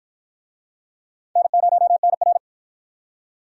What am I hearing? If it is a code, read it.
Morse code, heard as a keyed tone, about 51 wpm, 700 Hz. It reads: N0NR